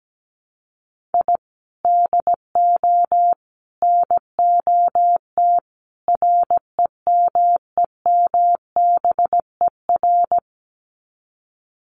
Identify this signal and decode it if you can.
Morse code, heard as a keyed tone, about 17 wpm, 700 Hz. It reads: I DO NOT REMEMBER
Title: Sets of 4 Words 17wpm